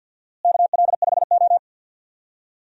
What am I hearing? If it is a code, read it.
Morse code, heard as a keyed tone, about 46 wpm, 700 Hz. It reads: KL5O
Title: USA Call Signs - 4 Characters - Repeated 3x - 50wpm